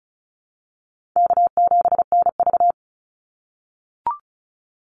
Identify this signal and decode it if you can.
Morse code, heard as a keyed tone, about 34 wpm, 700 Hz. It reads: K7NV
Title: Call Signs - 35wpm